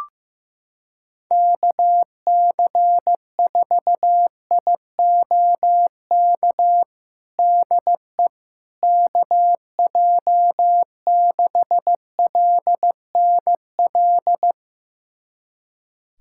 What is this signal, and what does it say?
Morse code, heard as a keyed tone, about 15 wpm, 700 Hz. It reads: KC4IOK DE KJ6LNL
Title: USA Callsign DE Callsign - 6 Characters - 15wpm